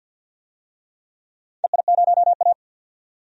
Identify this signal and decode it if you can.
Morse code, heard as a keyed tone, about 51 wpm, 700 Hz. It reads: EI0A
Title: DXCC-100 Call Signs - 4 Characters - 50wpm